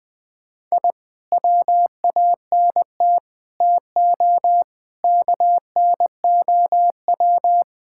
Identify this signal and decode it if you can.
Morse code, heard as a keyed tone, about 20 wpm, 700 Hz. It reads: I WANT TO KNOW
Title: Sentences with 2-Word Phrases - 20wpm